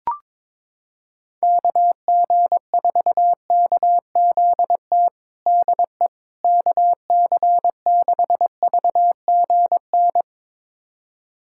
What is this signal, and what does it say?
Morse code, heard as a keyed tone, about 22 wpm, 700 Hz. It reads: KG4KZT DE KC6VGN
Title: USA Callsign DE Callsign 22wpm